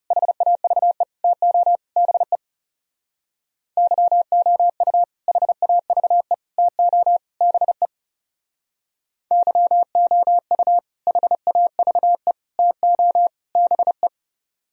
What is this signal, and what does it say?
Morse code, heard as a keyed tone, about 36 wpm, 700 Hz. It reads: HAVE TO BE YOU HAVE TO BE YOU HAVE TO BE
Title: Speed-Racing - Sets of 4 Words 30wpm